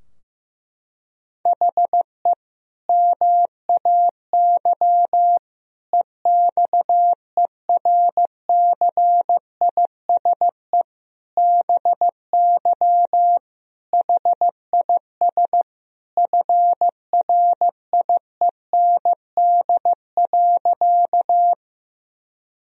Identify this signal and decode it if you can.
Morse code, heard as a keyed tone, about 16 wpm, 700 Hz. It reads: HE MAY EXERCISE BY HIS FRIEND.
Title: Sentences from Top 1000 Words - Review Part 4 - 15wpm